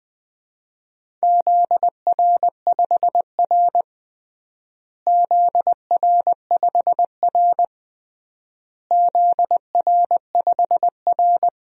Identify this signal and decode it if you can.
Morse code, heard as a keyed tone, about 20 wpm, 700 Hz. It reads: ZR5R ZR5R ZR5R
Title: DXCC-100 Call Signs - 4 Characters - Repeated 3x - 20wpm